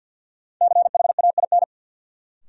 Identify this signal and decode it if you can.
Morse code, heard as a keyed tone, about 46 wpm, 700 Hz. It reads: CHAIN
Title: Sentences from Top 1500 Words - New Word Drill Alt - 50wpm